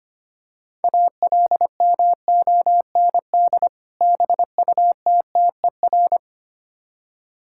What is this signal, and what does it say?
Morse code, heard as a keyed tone, about 25 wpm, 700 Hz. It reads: ALMOND BUTTER
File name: practice / 25wpm / Difficult Words with Hints - 25wpm